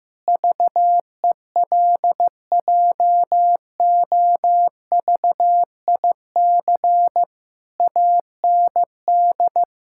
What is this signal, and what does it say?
Morse code, heard as a keyed tone, about 16 wpm, 700 Hz. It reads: VELJOVIC AND
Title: Pangram Challenge v2 - 15wpm